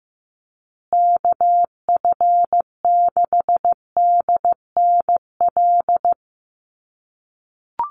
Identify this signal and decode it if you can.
Morse code, heard as a keyed tone, about 15 wpm, 700 Hz. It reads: KF6DNL
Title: USA Call Signs Repeated 3x 15wpm